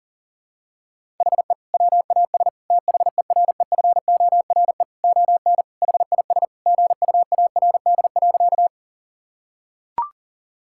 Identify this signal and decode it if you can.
Morse code, heard as a keyed tone, about 40 wpm, 700 Hz. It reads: HE WAS THEREFORE ON HIS GUARD.
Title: Sentences from Top 1000 Words - Review Part 5 - 40wpm